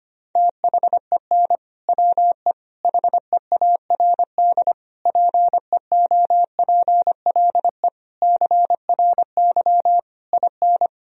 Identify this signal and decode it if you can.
Morse code, heard as a keyed tone, about 25 wpm, 700 Hz. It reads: THEN WE HEARD PEOPLE CRY IN
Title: Sentences from Top 1000 Words - Review Part 2 - 25wpm